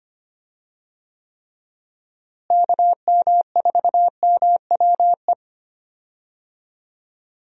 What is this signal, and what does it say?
Morse code, heard as a keyed tone, about 25 wpm, 700 Hz. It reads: KM4MWE
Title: USA Call Signs - 6 Characters - Repeated 3x - 25wpm